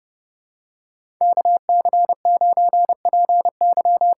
Morse code, heard as a keyed tone, about 30 wpm, 700 Hz. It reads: KC9PY
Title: USA Call Signs - 6 Characters - 8x Character Spacing - 30wpm